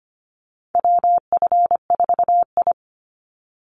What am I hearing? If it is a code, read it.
Morse code, heard as a keyed tone, about 25 wpm, 700 Hz. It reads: WF4I
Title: NA SOTA Chaser Callsigns 25wpm